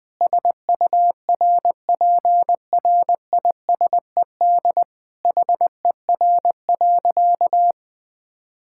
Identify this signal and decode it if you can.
Morse code, heard as a keyed tone, about 20 wpm, 700 Hz. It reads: SURPRISED HER.
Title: Sentences with 4-Word Phrases - 20wpm